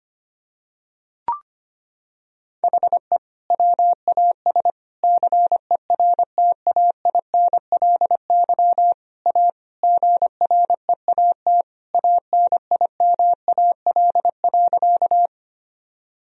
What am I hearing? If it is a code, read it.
Morse code, heard as a keyed tone, about 25 wpm, 700 Hz. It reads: HE WAS CERTAINLY A GREAT ANIMAL.
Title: Sentences from Top 1000 Words - Review Part 1 - 25wpm